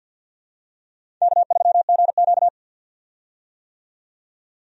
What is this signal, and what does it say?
Morse code, heard as a keyed tone, about 52 wpm, 700 Hz. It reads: K3GQ
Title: USA Call Signs - 4 Characters - 50wpm